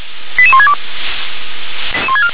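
Five-tone selective call (ZVEI), heard as quick runs of five tones, tones of about 70 ms. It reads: 88162
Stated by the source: ZVEI five-tone call received by SDR.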